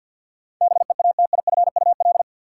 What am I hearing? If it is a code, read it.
Morse code, heard as a keyed tone, about 51 wpm, 700 Hz. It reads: BEATIFUL
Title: Sentences from Top 1300 Words - New Word Drill - 50wpm